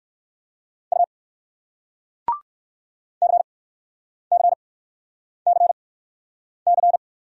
Morse code, heard as a keyed tone, about 60 wpm, spaced as for 9 wpm, 700 Hz. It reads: U CCCC